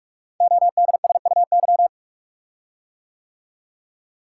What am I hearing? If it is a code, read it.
Morse code, heard as a keyed tone, about 44 wpm, 700 Hz. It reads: ODSUY
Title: Code Groups - 5 Letters - 45wpm